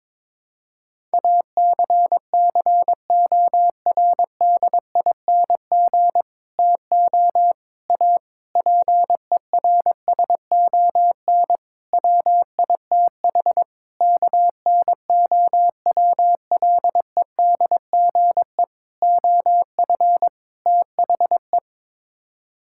Morse code, heard as a keyed tone, about 22 wpm, 700 Hz. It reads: ACCORDING TO A PERSON WITH KNOWLEDGE OF THE
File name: practice / 22wpm / Sets of 8 Words - 22wpm